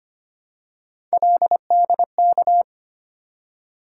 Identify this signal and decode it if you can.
Morse code, heard as a keyed tone, about 25 wpm, 700 Hz. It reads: LDK
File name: Three Letter-Number Combinations - 25wpm